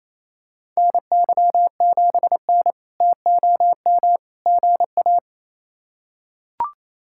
Morse code, heard as a keyed tone, about 28 wpm, 700 Hz. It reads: NY7N TOM GA
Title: K1USN SST QSO Practice - 28wpm